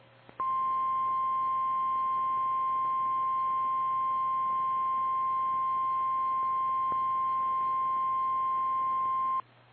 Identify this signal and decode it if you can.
Emergency Alert System (SAME) tones (the 1050 Hz attention tone). It is heard from 0.4 to 9.5 s.
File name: eas / KKPRKXPN / KKPRKXPN